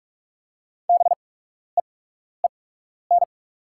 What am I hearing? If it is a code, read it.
Morse code, heard as a keyed tone, about 44 wpm, spaced as for 12 wpm, 700 Hz. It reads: BEEN
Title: 8x Character Spacing - Top 100 Words - 45wpm